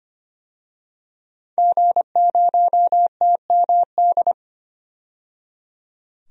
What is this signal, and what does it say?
Morse code, heard as a keyed tone, about 25 wpm, 700 Hz. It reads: G0TMD
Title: DXCC-100 Call Signs - 5 Characters - 25wpm